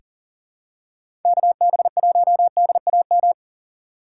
Morse code, heard as a keyed tone, about 40 wpm, 700 Hz. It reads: KB1DAM
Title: USA Call Signs - 6 Characters - 3x Character Spacing - 40wpm